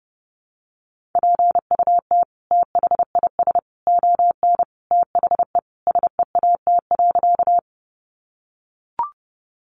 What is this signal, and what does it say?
Morse code, heard as a keyed tone, about 30 wpm, 700 Hz. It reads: PUT THIS ON THE SEAT.
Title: Sentences from Top 1500 Words 30wpm